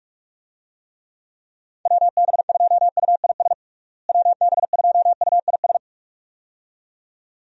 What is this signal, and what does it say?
Morse code, heard as a keyed tone, about 46 wpm, 700 Hz. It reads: WB2UIS WB2UIS
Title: USA Call Signs - 6 Characters - Repeated 2x - 45wpm